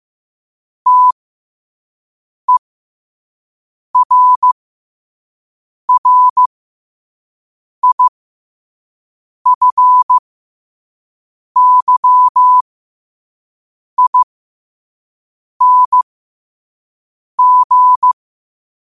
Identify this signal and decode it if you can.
Morse code, heard as a keyed tone, about 15 wpm, spaced as for 5 wpm, 1000 Hz. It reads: TERRIFYING
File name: qotdfiles / 5wpm